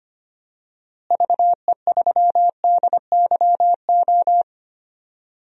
Morse code, heard as a keyed tone, about 25 wpm, 700 Hz. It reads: VE3DYO
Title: DXCC-100 Call Signs - 6 Characters - Repeated 3x - 25wpm